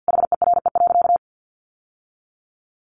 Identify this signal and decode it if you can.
Morse code, heard as a keyed tone, about 50 wpm, 700 Hz. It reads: HERE.
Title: Sentences from Top 600 Words 50wpm